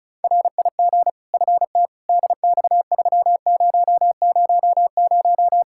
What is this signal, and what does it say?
Morse code, heard as a keyed tone, about 36 wpm, 700 Hz. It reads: RIG FT DX3000
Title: Practice QSO - Exchange 2 - Info - 35wpm